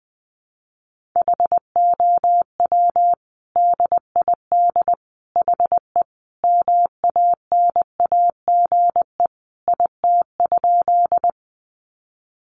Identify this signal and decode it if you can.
Morse code, heard as a keyed tone, about 20 wpm, 700 Hz. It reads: HOW DID HE MANAGE IT?
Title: Sentences from Top 1400 Words 20wpm